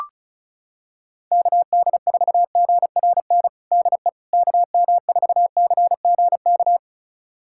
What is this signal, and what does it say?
Morse code, heard as a keyed tone, about 36 wpm, 700 Hz. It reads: KD4GRN DE KM4CGK
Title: USA Callsign DE Callsign - 6 Characters - 35wpm